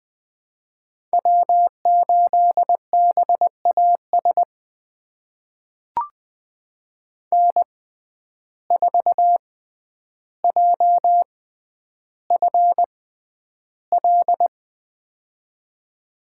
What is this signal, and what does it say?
Morse code, heard as a keyed tone, about 20 wpm, spaced as for 10 wpm, 700 Hz. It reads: W8BAS N4JFL
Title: USA Call Signs - 5 Characters - 6x Character Spacing - 20wpm